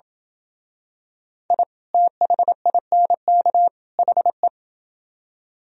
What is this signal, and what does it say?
Morse code, heard as a keyed tone, about 27 wpm, 700 Hz. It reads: I THINK HE